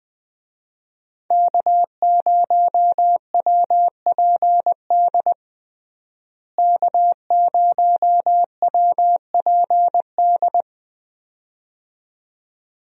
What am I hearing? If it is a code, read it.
Morse code, heard as a keyed tone, about 21 wpm, 700 Hz. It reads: K0WPD K0WPD
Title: USA Call Signs - 5 Characters - Repeated 2x - 20wpm